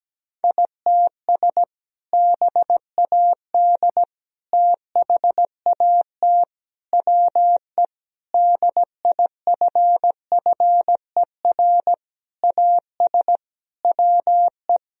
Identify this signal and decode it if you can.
Morse code, heard as a keyed tone, about 17 wpm, 700 Hz. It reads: ITS BAD THAT WE DIFFER AS WE
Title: Sentences from Top 1000 Words - Review Part 1 - 17wpm